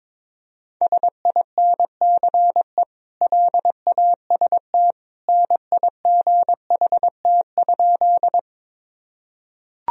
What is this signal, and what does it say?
Morse code, heard as a keyed tone, about 22 wpm, 700 Hz. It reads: SINCE LAST NIGHT?
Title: Sentences from Top 1000 Words - Review Part 1 - 22wpm